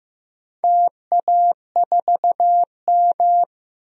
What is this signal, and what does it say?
Morse code, heard as a keyed tone, about 15 wpm, 700 Hz. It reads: TA4M
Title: DXCC-100 Call Signs - 4 Characters - Repeated 3x - 15wpm